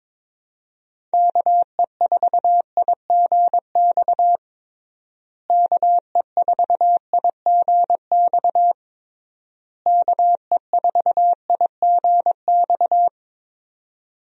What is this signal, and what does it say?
Morse code, heard as a keyed tone, about 22 wpm, 700 Hz. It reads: KE4IGX KE4IGX KE4IGX
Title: USA Call Signs - 6 Characters - Repeated 3x - 22wpm